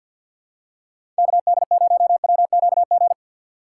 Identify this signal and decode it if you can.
Morse code, heard as a keyed tone, about 53 wpm, 700 Hz. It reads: KD0WQG
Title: USA Call Signs - 6 Characters - 6x Character Spacing - 50wpm